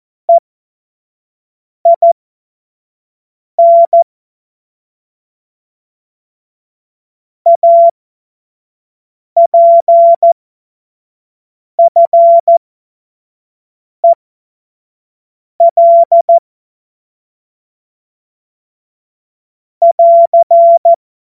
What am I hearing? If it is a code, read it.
Morse code, heard as a keyed tone, about 14 wpm, spaced as for 5 wpm, 680 Hz. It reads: EIN APFEL <AR>
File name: Witz-16-September-2025-25-F70